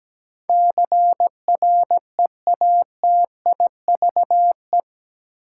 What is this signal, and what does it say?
Morse code, heard as a keyed tone, about 17 wpm, 700 Hz. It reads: CREATIVE
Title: Sentences from Top 1000 Words - Review Part 5 - New Word Drill - 17wpm